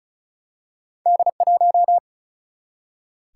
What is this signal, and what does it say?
Morse code, heard as a keyed tone, about 36 wpm, 700 Hz. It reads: D 1
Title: Single-Letter Single-Number 35wpm